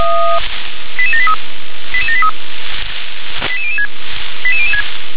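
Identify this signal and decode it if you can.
Five-tone selective call (ZVEI), heard as quick runs of five tones, tones of about 70 ms. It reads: 88783 88783 88006 88006
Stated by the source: ZVEI five-tone selective call 88083 received via SDR.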